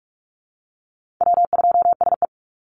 Morse code, heard as a keyed tone, about 44 wpm, 690 Hz. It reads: W2SE
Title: NA SOTA Chaser Callsigns 45wpm